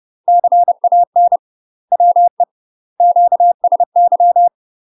Morse code, heard as a keyed tone, about 30 wpm, 700 Hz. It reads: CAN WE QSY
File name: UK MASTER QSOs TRAINING / 30WPM / UK MASTER QSOs 0018